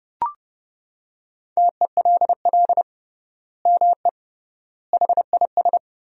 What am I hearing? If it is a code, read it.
Morse code, heard as a keyed tone, about 30 wpm, 700 Hz. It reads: TELL ME HIS
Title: 3x Word-Spacing - Sentences from Top 200 Words - 30wpm